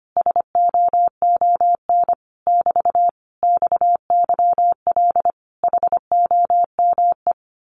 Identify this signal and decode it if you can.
Morse code, heard as a keyed tone, about 25 wpm, 700 Hz. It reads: SOON <BT> XYL HOME
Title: Common QSO Phrases Encore 25wpm